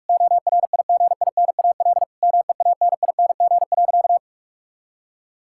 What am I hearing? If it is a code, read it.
Morse code, heard as a keyed tone, about 44 wpm, 700 Hz. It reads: ORIGINAL MEANING.
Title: Sentences from Top 1000 Words - Review Part 6 - 45wpm